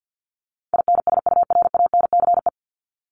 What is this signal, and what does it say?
Morse code, heard as a keyed tone, about 50 wpm, 700 Hz. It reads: INSURANCE
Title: Top 1001-1250 Words 50wpm